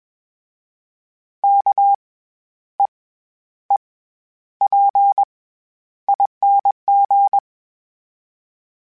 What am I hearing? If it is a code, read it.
Morse code, heard as a keyed tone, about 20 wpm, spaced as for 10 wpm, 800 Hz. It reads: KEEPING
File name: Special INGs-mp3s / Special INGs-mp3s / SlowToFaster ing @20